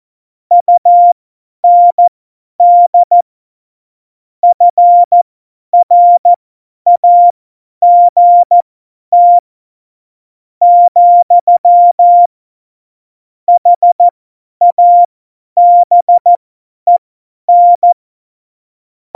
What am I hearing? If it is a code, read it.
Morse code, heard as a keyed tone, about 14 wpm, spaced as for 9 wpm, 700 Hz. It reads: UND FRAGT, HABEN
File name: Morsekurs / Witze / Witz 30. Oktober 2018, 50 F70